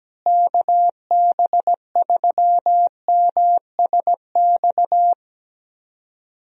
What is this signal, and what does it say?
Morse code, heard as a keyed tone, about 17 wpm, 700 Hz. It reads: KB3MSX
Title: USA Call Signs - 6 Characters - Repeated 3x - 17wpm